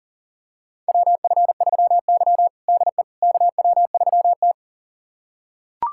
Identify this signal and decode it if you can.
Morse code, heard as a keyed tone, about 37 wpm, 700 Hz. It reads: WF3Y DE KW3T
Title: USA Callsign DE Callsign - 4 Characters - 40wpm